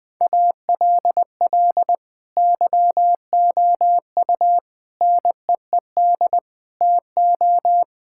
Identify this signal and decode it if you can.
Morse code, heard as a keyed tone, about 20 wpm, 700 Hz. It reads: ALL YOU NEED TO
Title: Sentences with 6-Word Phrases - 20wpm